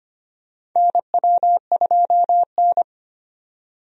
Morse code, heard as a keyed tone, about 25 wpm, 700 Hz. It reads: NW2N
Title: USA Call Signs - 4 Characters - 25wpm